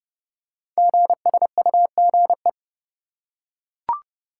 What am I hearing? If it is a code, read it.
Morse code, heard as a keyed tone, about 28 wpm, 700 Hz. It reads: GSUGE
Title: Code Groups - 5 Letters - 30wpm